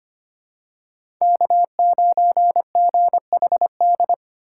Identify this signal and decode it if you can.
Morse code, heard as a keyed tone, about 25 wpm, 700 Hz. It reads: K9GHD
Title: USA Call Signs - 5 Characters - 4x Character Spacing - 25wpm